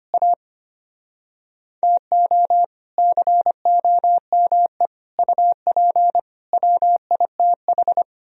Morse code, heard as a keyed tone, about 33 wpm, 700 Hz. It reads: A TO COME UP WITH